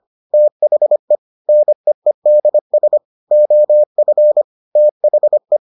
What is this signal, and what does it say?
Morse code, heard as a keyed tone, about 25 wpm, 600 Hz. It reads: THE NEEDS OF THE